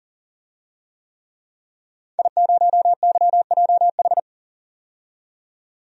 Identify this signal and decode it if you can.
Morse code, heard as a keyed tone, about 40 wpm, 700 Hz. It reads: I0YJH
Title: DXCC-100 Call Signs - 5 Characters - 40wpm